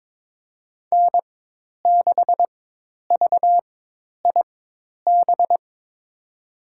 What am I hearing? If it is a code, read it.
Morse code, heard as a keyed tone, about 22 wpm, spaced as for 10 wpm, 700 Hz. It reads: N6VIB
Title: USA Call Signs - 5 Characters - 4x Character Spacing - 22wpm